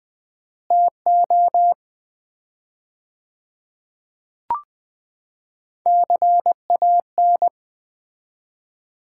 Morse code, heard as a keyed tone, about 20 wpm, 700 Hz. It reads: TO CAN
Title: 4x Word-Spacing - Sets of 2 Words - 20wpm